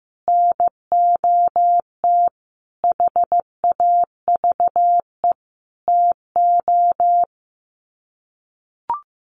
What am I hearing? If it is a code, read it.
Morse code, heard as a keyed tone, about 15 wpm, 700 Hz. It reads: NOT HAVE TO
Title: Sets of 5 Words 15wpm